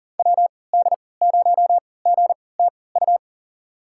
Morse code, heard as a keyed tone, about 40 wpm, spaced as for 23 wpm, 700 Hz. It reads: WD0GTU
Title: USA Call Signs - 6 Characters - 3x Character Spacing - 40wpm